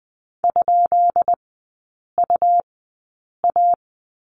Morse code, heard as a keyed tone, about 20 wpm, spaced as for 8 wpm, 700 Hz. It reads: ?UA
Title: 159 Lesson 26e - Letter g - Three Characters 20wpm